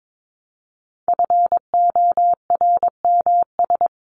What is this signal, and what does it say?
Morse code, heard as a keyed tone, about 22 wpm, 700 Hz. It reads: FORMS
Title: ES:S Suffix - Top 100 Words with Base Words 22wpm